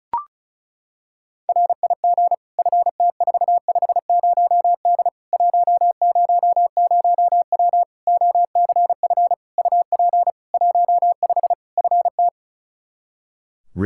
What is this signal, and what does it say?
Morse code, heard as a keyed tone, about 36 wpm, 700 Hz. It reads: RIG FT450D 100W OCF UP 15 FT
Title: Practice QSO - Exchange 2 - Info - 35wpm